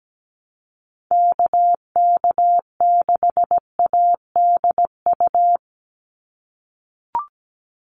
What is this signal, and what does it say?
Morse code, heard as a keyed tone, about 17 wpm, 700 Hz. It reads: KK6ADU
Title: USA Call Signs 17wpm